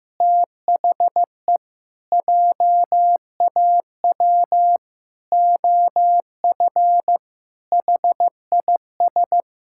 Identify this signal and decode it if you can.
Morse code, heard as a keyed tone, about 15 wpm, 700 Hz. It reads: THE JAW OF HIS
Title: Pangram Challenge v2 - 15wpm